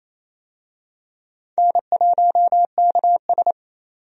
Morse code, heard as a keyed tone, about 28 wpm, 700 Hz. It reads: N1KS
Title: USA Call Signs - 4 Characters - 28wpm